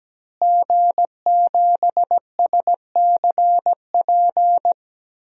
Morse code, heard as a keyed tone, about 17 wpm, 700 Hz. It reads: G7SCP
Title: DXCC-100 Call Signs - 5 Characters - Repeated 2x - 17wpm